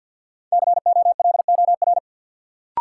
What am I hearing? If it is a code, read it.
Morse code, heard as a keyed tone, about 52 wpm, 700 Hz. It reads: COLOR
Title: Sentences from Top 1600 Words - New Word Drill Alt - 50wpm